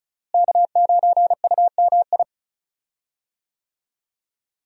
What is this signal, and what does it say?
Morse code, heard as a keyed tone, about 34 wpm, 700 Hz. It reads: K9UMI
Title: USA Call Signs - 5 Characters - Repeated 3x - 35wpm